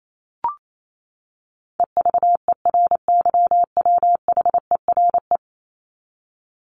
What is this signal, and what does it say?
Morse code, heard as a keyed tone, about 28 wpm, 700 Hz. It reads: EVERYWHERE
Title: Top 1501-1750 Words 28wpm